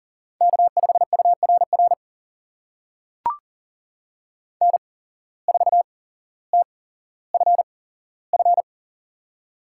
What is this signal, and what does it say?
Morse code, heard as a keyed tone, about 43 wpm, spaced as for 17 wpm, 700 Hz. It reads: K5URR N4TFF
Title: USA Call Signs - 5 Characters - 8x Character Spacing - 40wpm